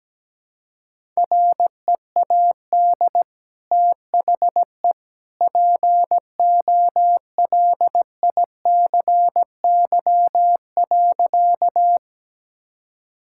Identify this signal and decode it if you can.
Morse code, heard as a keyed tone, about 17 wpm, 700 Hz. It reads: READ THE POLICY.
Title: Sentences from Top 1000 Words - Review Part 5 - 17wpm